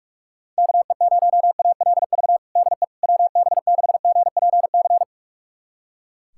Morse code, heard as a keyed tone, about 41 wpm, 700 Hz. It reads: KE0ALV DE WB6GPC
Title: USA Callsign DE Callsign - 6 Characters - 45wpm